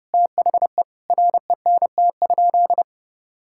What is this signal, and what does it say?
Morse code, heard as a keyed tone, about 30 wpm, 700 Hz. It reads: THE RENT?
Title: Sentences from Top 1000 Words - Review Part 5 - 30wpm